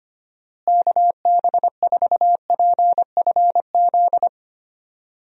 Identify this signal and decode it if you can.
Morse code, heard as a keyed tone, about 25 wpm, 700 Hz. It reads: KB4PFZ
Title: USA Call Signs - 6 Characters - Repeated 2x - 25wpm